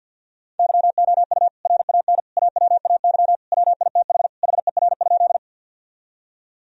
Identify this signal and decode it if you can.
Morse code, heard as a keyed tone, about 50 wpm, 700 Hz. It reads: YOU RAN AWAY WITH HER?
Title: Sentences from Top 1000 Words - Review Part 3 - 50wpm